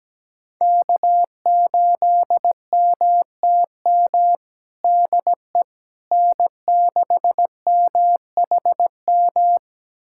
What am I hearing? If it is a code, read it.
Morse code, heard as a keyed tone, about 17 wpm, 700 Hz. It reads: K8MTM DE N6MHM
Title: USA Callsign DE Callsign - 5 Characters - 17wpm